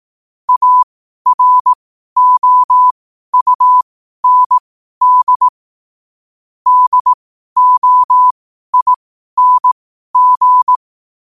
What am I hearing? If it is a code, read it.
Morse code, heard as a keyed tone, about 18 wpm, spaced as for 12 wpm, 1000 Hz. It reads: AROUND DOING
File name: qotdfiles / 12wpm